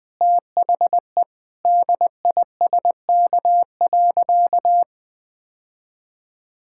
Morse code, heard as a keyed tone, about 20 wpm, 700 Hz. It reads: THE DISK.
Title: Sentences from Top 1000 Words - Review Part 6 - 20wpm